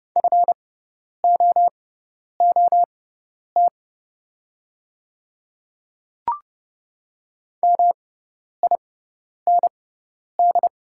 Morse code, heard as a keyed tone, about 31 wpm, spaced as for 10 wpm, 700 Hz. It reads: FOOT MIND
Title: 6x Character Spacing - Top 500 Words - 30wpm